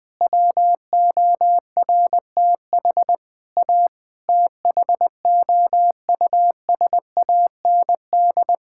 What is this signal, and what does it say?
Morse code, heard as a keyed tone, about 20 wpm, 700 Hz. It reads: WORTH A THOUSAND
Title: Sentences with 7-Word Phrases - Spoken Beginning - 20wpm